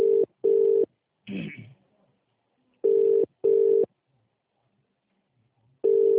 Ringback tone (UK), 400 + 450 Hz.